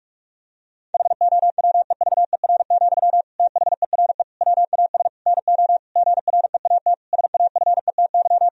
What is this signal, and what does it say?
Morse code, heard as a keyed tone, about 46 wpm, 700 Hz. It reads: HOWEVER, THERE WAS NO GREAT SAFETY
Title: Sentences from Top 1000 Words - Review Part 4 - 45wpm